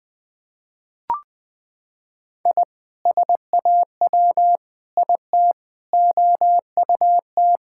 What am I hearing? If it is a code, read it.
Morse code, heard as a keyed tone, about 20 wpm, 700 Hz. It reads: I SAW IT OUT
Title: Sentences with 7-Word Phrases - 20wpm